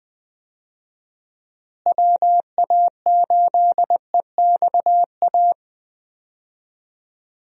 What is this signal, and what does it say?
Morse code, heard as a keyed tone, about 20 wpm, 700 Hz. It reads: WA8EXA
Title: USA Call Signs - 6 Characters - Repeated 2x - 20wpm